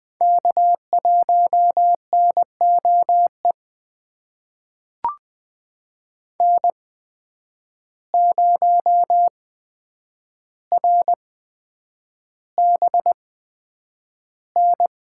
Morse code, heard as a keyed tone, about 20 wpm, spaced as for 9 wpm, 700 Hz. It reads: K1NOE N0RBN
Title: USA Call Signs - 5 Characters - 8x Character Spacing - 20wpm